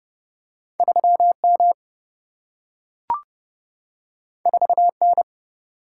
Morse code, heard as a keyed tone, about 30 wpm, 700 Hz. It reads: 3 M 4 N
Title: Single-Letter Single-Number 30wpm